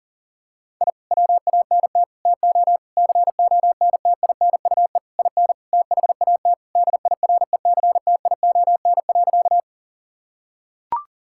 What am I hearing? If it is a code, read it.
Morse code, heard as a keyed tone, about 40 wpm, 700 Hz. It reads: I WANT TO CONTINUE IN THAT DIRECTION.
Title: Sentences from Top 1000 Words - Review Part 6 - 40wpm